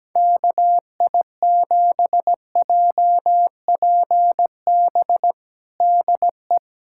Morse code, heard as a keyed tone, about 17 wpm, 700 Hz. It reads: KI7JPB DE
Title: USA Callsign DE Callsign 17wpm